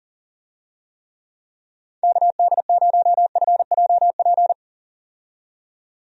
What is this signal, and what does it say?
Morse code, heard as a keyed tone, about 40 wpm, 700 Hz. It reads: KD0FJP
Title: USA Call Signs - 6 Characters - 40wpm